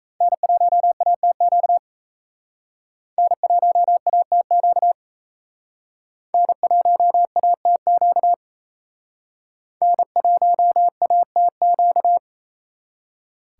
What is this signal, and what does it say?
Morse code, heard as a keyed tone, about 35 wpm, 700 Hz. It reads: N1ATQ N1ATQ N1ATQ N1ATQ
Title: Speed Racing - USA Call Signs 28wpm